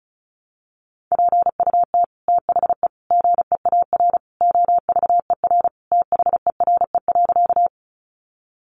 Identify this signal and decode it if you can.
Morse code, heard as a keyed tone, about 34 wpm, 700 Hz. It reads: PUT THE GEAR OVER THERE.
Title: Sentences from Top 1400 Words 35wpm